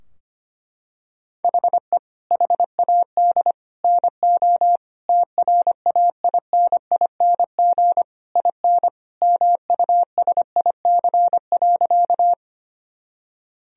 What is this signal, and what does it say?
Morse code, heard as a keyed tone, about 25 wpm, 700 Hz. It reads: HE HAD NO TRAINING IN MUSIC.
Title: Sentences from Top 1000 Words - Review Part 4 - 25wpm